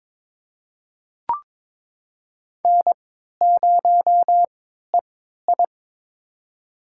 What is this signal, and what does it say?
Morse code, heard as a keyed tone, about 22 wpm, spaced as for 13 wpm, 700 Hz. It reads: N0EI
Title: USA Call Signs - 4 Characters - 3x Character Spacing - 22wpm